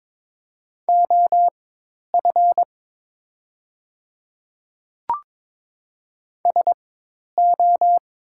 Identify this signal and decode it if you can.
Morse code, heard as a keyed tone, about 22 wpm, spaced as for 10 wpm, 700 Hz. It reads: OF SO
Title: 4x Character Spacing - Top 300 Words - 22wpm